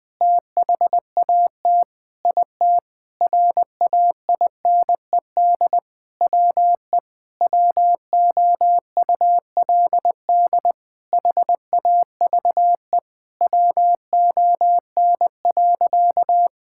Morse code, heard as a keyed tone, about 21 wpm, 700 Hz. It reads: THAT IT RAINED WE WOULD HAVE WON.
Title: Sentences with 7-Word Phrases - Spoken Beginning - 20wpm